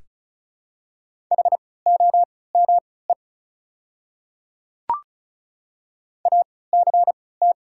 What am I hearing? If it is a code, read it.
Morse code, heard as a keyed tone, about 36 wpm, spaced as for 20 wpm, 700 Hz. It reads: HOME ACT
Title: 3x Character Spacing - Top 200 Words - 35wpm